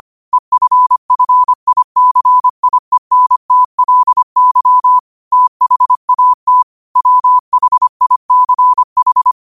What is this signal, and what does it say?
Morse code, heard as a keyed tone, about 25 wpm, 1000 Hz. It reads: EFFICIENTLY THAT WHICH
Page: New quotes every day in morse code at 25 Words per minute.